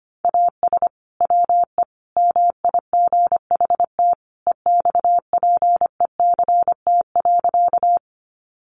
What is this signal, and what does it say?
Morse code, heard as a keyed tone, about 25 wpm, 700 Hz. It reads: AS WE MIGHT EXPECT.
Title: Sentences from Top 1100 Words 25wpm